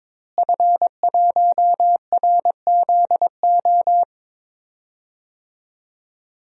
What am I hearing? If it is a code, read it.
Morse code, heard as a keyed tone, about 22 wpm, 700 Hz. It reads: F1RZO
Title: DXCC-100 Call Signs - 5 Characters - Repeated 2x - 22wpm